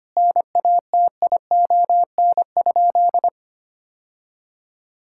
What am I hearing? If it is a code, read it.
Morse code, heard as a keyed tone, about 25 wpm, 700 Hz. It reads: NATION?
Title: Sentences from Top 1000 Words - Review Part 6 - 25wpm